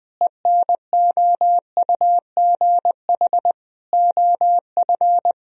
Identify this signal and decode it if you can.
Morse code, heard as a keyed tone, about 20 wpm, 700 Hz. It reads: ENOUGH OF
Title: Sentences from Top 1000 Words - Review Part 3 - 20wpm